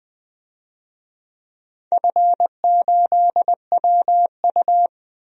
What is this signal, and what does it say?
Morse code, heard as a keyed tone, about 20 wpm, 700 Hz. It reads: F8WU
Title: DXCC-100 Call Signs - 4 Characters - Repeated 3x - 20wpm